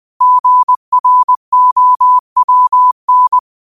Morse code, heard as a keyed tone, about 20 wpm, 1000 Hz. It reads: GROWN
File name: qotdfiles / 20wpm